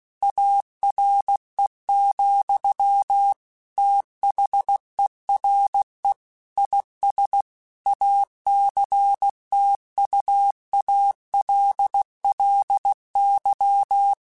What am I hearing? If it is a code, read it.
Morse code, heard as a keyed tone, about 16 wpm, 800 Hz. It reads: ARE, THERE IS ACTUALLY
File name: Technical Subjects / Technical Subjects-mp3s / AND Gate @15